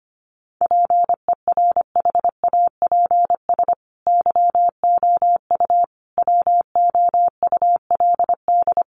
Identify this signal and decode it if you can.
Morse code, heard as a keyed tone, about 25 wpm, 700 Hz. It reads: PERHAPS YOU WOULD
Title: Sentences from Top 1100 Words 25wpm